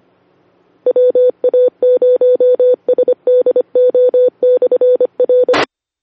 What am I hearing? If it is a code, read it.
Morse code, heard as a keyed tone, about 25 wpm, 500 Hz. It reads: WA0SDO/R
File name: sedaliarepeater